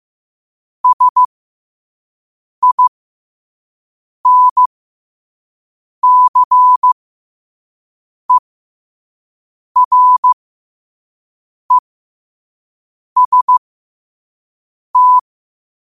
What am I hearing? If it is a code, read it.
Morse code, heard as a keyed tone, about 15 wpm, spaced as for 5 wpm, 1000 Hz. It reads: SINCEREST
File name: qotdfiles / 5wpm